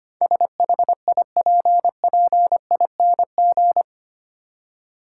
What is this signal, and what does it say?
Morse code, heard as a keyed tone, about 25 wpm, 700 Hz. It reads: SHIPPING
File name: Eight Letter Words - 25wpm